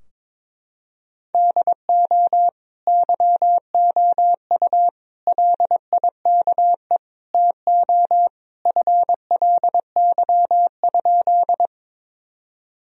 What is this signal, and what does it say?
Morse code, heard as a keyed tone, about 22 wpm, 700 Hz. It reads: DO YOU LIKE TO FLY?
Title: Sentences from Top 1000 Words - Review Part 2 - 22wpm